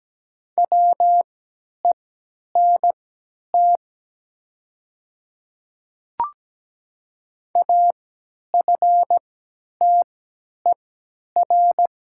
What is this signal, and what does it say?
Morse code, heard as a keyed tone, about 17 wpm, spaced as for 10 wpm, 700 Hz. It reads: WENT AFTER
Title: 3x Character Spacing - Top 300 Words - 17wpm